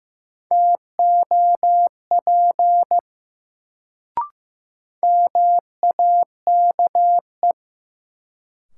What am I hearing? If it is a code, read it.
Morse code, heard as a keyed tone, about 15 wpm, 700 Hz. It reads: TOP MAKE
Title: Sets of 2 Words - New Word Drill - 15wpm